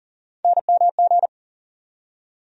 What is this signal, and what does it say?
Morse code, heard as a keyed tone, about 39 wpm, 700 Hz. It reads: NMG
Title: USA Call Sign Postfixes - 3 Letters 40wpm